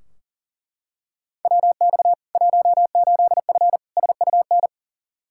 Morse code, heard as a keyed tone, about 40 wpm, 700 Hz. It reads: WX 18F SUN
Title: Practice QSO - Exchange 2 - Info - 40wpm